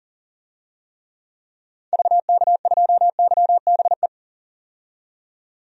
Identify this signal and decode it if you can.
Morse code, heard as a keyed tone, about 37 wpm, 700 Hz. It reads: VK2YBE
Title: DXCC-100 Call Signs - 6 Characters - 40wpm